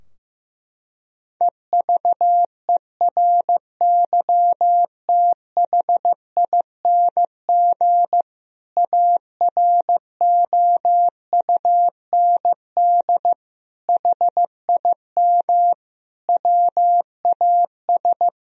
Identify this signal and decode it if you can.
Morse code, heard as a keyed tone, about 15 wpm, 700 Hz. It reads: EVERYTHING AROUND HIM WAS
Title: Sentences from Top 1000 Words - Review Part 3 - 15wpm